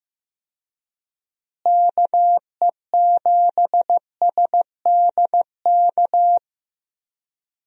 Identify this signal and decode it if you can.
Morse code, heard as a keyed tone, about 15 wpm, 700 Hz. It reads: KE7SDK
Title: USA Call Signs - 6 Characters - Repeated 2x - 15wpm